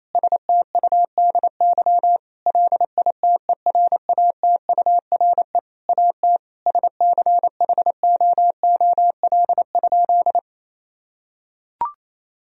Morse code, heard as a keyed tone, about 28 wpm, 700 Hz. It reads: STUDY LITERATURE AT SCHOOL?
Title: Sentences from Top 1000 Words - Review Part 3 - 28wpm